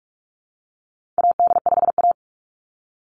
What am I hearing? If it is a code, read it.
Morse code, heard as a keyed tone, about 46 wpm, 700 Hz. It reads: AD5A
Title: Call Signs - 45wpm